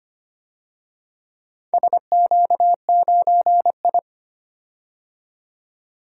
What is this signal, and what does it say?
Morse code, heard as a keyed tone, about 25 wpm, 700 Hz. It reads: SQ9I
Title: DXCC-100 Call Signs - 4 Characters - 25wpm